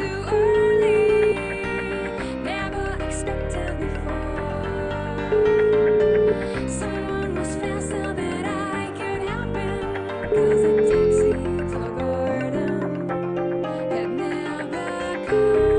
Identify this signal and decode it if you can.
Ringback tone (Europe, 425 Hz), with noise.